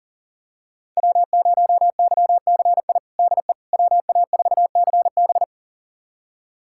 Morse code, heard as a keyed tone, about 38 wpm, 700 Hz. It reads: W0YCI DE WA4CB
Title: USA Callsign DE Callsign - 5 Characters - 40wpm